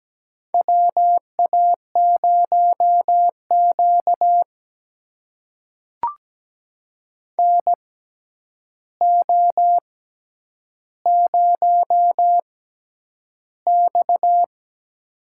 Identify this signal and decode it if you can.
Morse code, heard as a keyed tone, about 17 wpm, spaced as for 9 wpm, 700 Hz. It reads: WA0Q NO0X
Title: USA Call Signs - 4 Characters - 6x Character Spacing - 17wpm